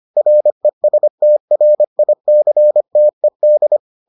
Morse code, heard as a keyed tone, about 25 wpm, 600 Hz. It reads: RESTRICTED